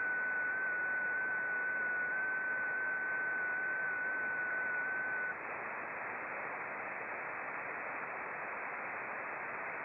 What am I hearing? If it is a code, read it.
FT8.
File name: contests / sept-vhf-k5tr- -0300